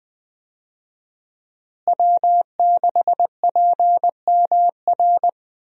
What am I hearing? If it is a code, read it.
Morse code, heard as a keyed tone, about 20 wpm, 700 Hz. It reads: W6PMR
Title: USA Call Signs - 5 Characters - Repeated 3x - 20wpm